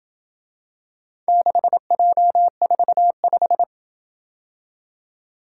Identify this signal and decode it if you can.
Morse code, heard as a keyed tone, about 27 wpm, 700 Hz. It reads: 6J45